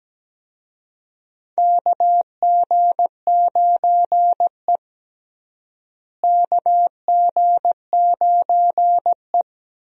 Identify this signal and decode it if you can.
Morse code, heard as a keyed tone, about 17 wpm, 700 Hz. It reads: KG9E KG9E
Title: USA Call Signs - 4 Characters - Repeated 2x - 17wpm